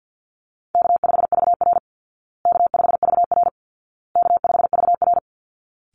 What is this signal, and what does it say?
Morse code, heard as a keyed tone, about 50 wpm, 700 Hz. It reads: K5VR K5VR K5VR
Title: Easy Call Signs Repeated 3x - 50wpm